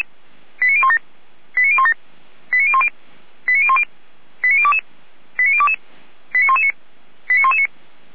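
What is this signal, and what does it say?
Five-tone selective call (ZVEI), heard as quick runs of five tones, tones of about 70 ms. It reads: 89017 89017 89010 89010 89022 89022 89119 89119
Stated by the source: Received on an SDR, ZVEI five-tone selective call.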